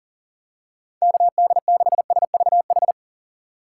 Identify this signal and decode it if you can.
Morse code, heard as a keyed tone, about 40 wpm, 700 Hz. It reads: KD6SVH
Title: USA Call Signs - 6 Characters - 2x Character Spacing - 40wpm